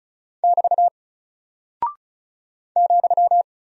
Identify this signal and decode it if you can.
Morse code, heard as a keyed tone, about 36 wpm, 700 Hz. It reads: <BT> ,
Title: Single Character (Letter:Number:Punctuation:Prosign) Alt 35wpm